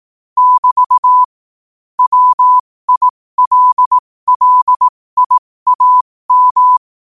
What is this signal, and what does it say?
Morse code, heard as a keyed tone, about 18 wpm, spaced as for 15 wpm, 1000 Hz. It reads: <BT> WILLIAM
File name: qotdfiles / 15wpm